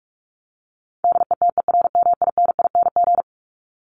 Morse code, heard as a keyed tone, about 44 wpm, 700 Hz. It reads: DETERMINING
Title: ING Suffix - Top 100 Words with Base Words 45wpm